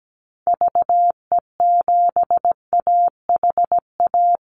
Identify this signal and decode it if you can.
Morse code, heard as a keyed tone, about 17 wpm, 700 Hz. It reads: VE7AHA
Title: Call Signs Repeated 2x - 17wpm